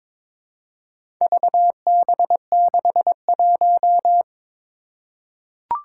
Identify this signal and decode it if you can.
Morse code, heard as a keyed tone, about 22 wpm, 700 Hz. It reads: VB61
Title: Four Challenging Letter-Number Combinations 22wpm